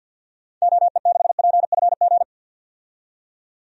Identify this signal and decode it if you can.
Morse code, heard as a keyed tone, about 50 wpm, 700 Hz. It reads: OE6PFG
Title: DXCC-100 Call Signs - 6 Characters - 50wpm